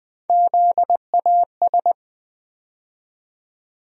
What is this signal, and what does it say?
Morse code, heard as a keyed tone, about 21 wpm, 700 Hz. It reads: ZAS
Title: USA Call Sign Postfixes - 3 Letters 20wpm